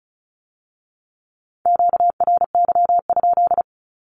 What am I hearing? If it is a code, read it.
Morse code, heard as a keyed tone, about 36 wpm, 700 Hz. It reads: QRY?
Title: Q-Codes with Spoken Meaning - 35wpm